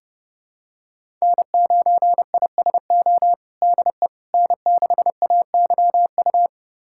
Morse code, heard as a keyed tone, about 31 wpm, 700 Hz. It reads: N9ISO DE N6AYU
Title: USA Callsign DE Callsign - 5 Characters - 30wpm